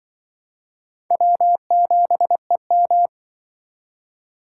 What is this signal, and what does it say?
Morse code, heard as a keyed tone, about 24 wpm, 700 Hz. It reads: W7EM
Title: Speed Racing - USA 1x2 Call Signs 20wpm